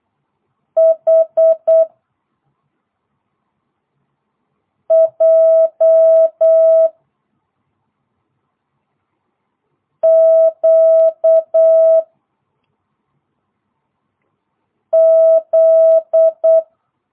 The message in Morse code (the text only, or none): H J Q Z